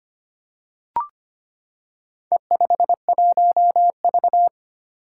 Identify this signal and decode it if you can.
Morse code, heard as a keyed tone, about 25 wpm, 700 Hz. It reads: E51V
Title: Four Challenging Letter-Number Combinations 25wpm